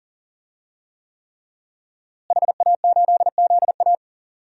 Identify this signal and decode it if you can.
Morse code, heard as a keyed tone, about 40 wpm, 700 Hz. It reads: HA8ZA
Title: DXCC-100 Call Signs - 5 Characters - Repeated 3x - 40wpm